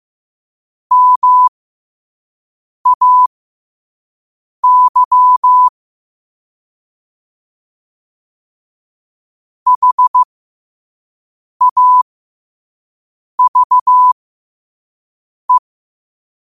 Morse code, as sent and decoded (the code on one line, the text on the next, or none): -- .- -.-- / .... .- ...- .
MAY HAVE